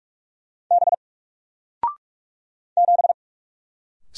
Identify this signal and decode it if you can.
Morse code, heard as a keyed tone, about 44 wpm, 700 Hz. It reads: B 7
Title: Single Letter-Number 45wpm